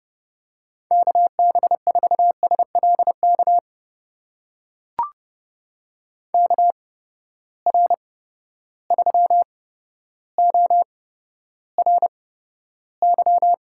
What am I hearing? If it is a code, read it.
Morse code, heard as a keyed tone, about 30 wpm, spaced as for 13 wpm, 700 Hz. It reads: KB4SLK KR3ORY
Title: USA Call Signs - 6 Characters - 8x Character Spacing - 30wpm